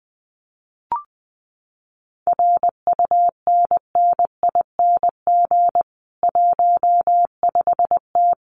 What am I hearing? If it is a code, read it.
Morse code, heard as a keyed tone, about 20 wpm, 700 Hz. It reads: RUNNING 15T
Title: Common QSO Phrases 20wpm